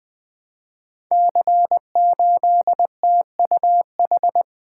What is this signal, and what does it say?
Morse code, heard as a keyed tone, about 20 wpm, 700 Hz. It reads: C8TUH
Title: Code Groups - 5 Letter:Numbers - 20wpm